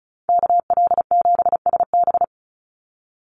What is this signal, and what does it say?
Morse code, heard as a keyed tone, about 34 wpm, 700 Hz. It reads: KL7SB
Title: Call Signs - 35wpm